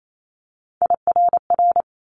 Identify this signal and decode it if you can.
Morse code, heard as a keyed tone, about 28 wpm, 700 Hz. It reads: IRR
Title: Multi-letter Phonemes Quick Fire 28wpm